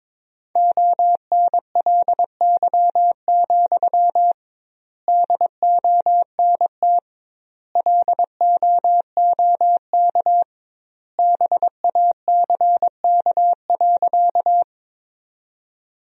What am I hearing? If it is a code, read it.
Morse code, heard as a keyed tone, about 22 wpm, 700 Hz. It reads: ONLY, DONT LOOK BACK.
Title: 2x Word-Spacing - Sentences from Top 300 Words - 22wpm